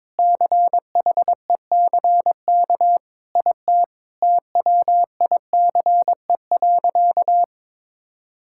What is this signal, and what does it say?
Morse code, heard as a keyed tone, about 22 wpm, 700 Hz. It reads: CHECK IT TWICE.
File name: practice / 22wpm / Sentences with 3-Word Phrases - 22wpm